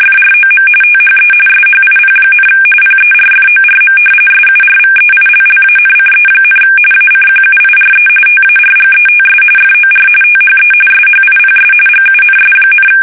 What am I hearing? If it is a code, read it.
RTTY, about 45.45 baud. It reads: FROM THE SHIP, PERFORM SURVEILLANCE, ACQUIRE TARGETS FOR THE SHIPS GUNS,
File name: fsk850 / battleshipnj